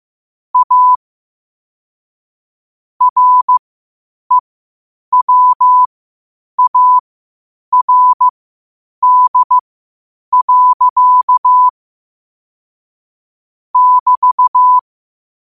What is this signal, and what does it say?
Morse code, heard as a keyed tone, about 15 wpm, spaced as for 8 wpm, 1000 Hz. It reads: A REWARD. <BT>
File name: qotdfiles / 8wpm